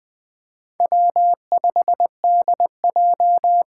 Morse code, heard as a keyed tone, about 20 wpm, 700 Hz. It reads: W5DJ
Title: Four Challenging Letter-Number Combinations 20wpm